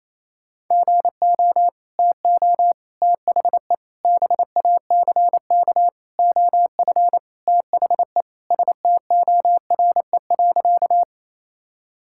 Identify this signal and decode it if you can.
Morse code, heard as a keyed tone, about 27 wpm, 700 Hz. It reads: GO TO THE BACK OF THE STORE.
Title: Sentences from Top 1000 Words - Review Part 5 - 28wpm